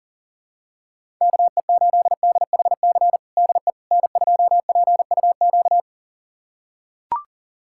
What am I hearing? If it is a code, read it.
Morse code, heard as a keyed tone, about 40 wpm, 700 Hz. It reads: KE8DHC DE N2PUQ
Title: USA Callsign DE Callsign 40wpm